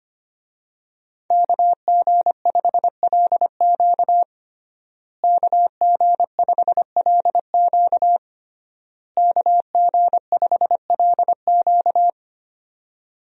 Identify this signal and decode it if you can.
Morse code, heard as a keyed tone, about 25 wpm, 700 Hz. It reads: KG5LQ KG5LQ KG5LQ
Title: USA Call Signs - 5 Characters - Repeated 3x - 25wpm